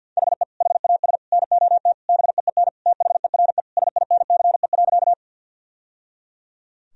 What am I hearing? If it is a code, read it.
Morse code, heard as a keyed tone, about 52 wpm, 700 Hz. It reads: HE HAS NOT BEEN THERE SINCE.
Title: Sentences from Top 1000 Words - Review Part 1 - 50wpm